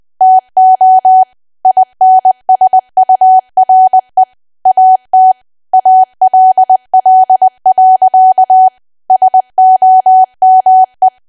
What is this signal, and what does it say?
Morse code, heard as a keyed tone, about 20 wpm, 750 Hz. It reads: TO INSURE AT ALL. SOME
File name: cwa / W1AW-Level3 / 20WPM